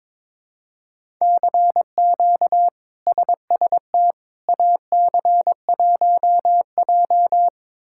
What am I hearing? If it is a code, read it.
Morse code, heard as a keyed tone, about 22 wpm, 700 Hz. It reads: CQ SST AC1J
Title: K1USN SST QSO Practice - 22wpm